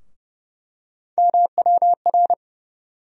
Morse code, heard as a keyed tone, about 30 wpm, 700 Hz. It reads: MWR
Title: USA Call Sign Postfixes - 3 Letters 30wpm